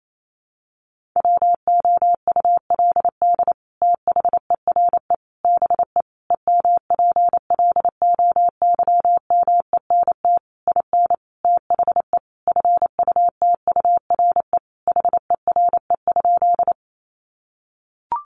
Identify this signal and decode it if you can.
Morse code, heard as a keyed tone, about 28 wpm, 700 Hz. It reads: WOULD THERE BE EMPLOYMENT IN THE FUTURE HERE?
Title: Sentences from Top 1500 Words 28wpm